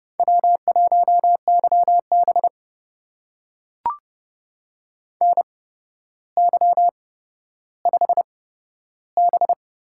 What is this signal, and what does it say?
Morse code, heard as a keyed tone, about 32 wpm, spaced as for 13 wpm, 700 Hz. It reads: W1YB NY5B
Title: USA Call Signs - 4 Characters - 8x Character Spacing - 30wpm